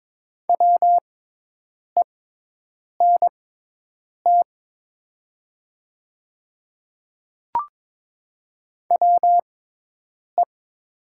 Morse code, heard as a keyed tone, about 22 wpm, spaced as for 8 wpm, 700 Hz. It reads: WENT WE
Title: 6x Character Spacing - Top 500 Words - 22wpm